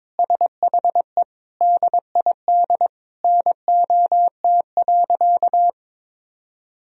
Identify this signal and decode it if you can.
Morse code, heard as a keyed tone, about 22 wpm, 700 Hz. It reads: SHE DID NOT.
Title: Sentences from Top 1000 Words - Review Part 1 - 22wpm